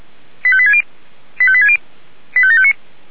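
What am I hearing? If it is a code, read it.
Five-tone selective call (ZVEI), heard as quick runs of five tones, tones of about 70 ms. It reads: 85860 85860 85759
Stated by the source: ZVEI five-tone selective call 85724 received by SDR.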